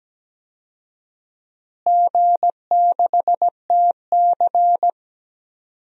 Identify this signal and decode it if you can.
Morse code, heard as a keyed tone, about 17 wpm, 700 Hz. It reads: G6TC
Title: DXCC-100 Call Signs - 4 Characters - Repeated 2x - 17wpm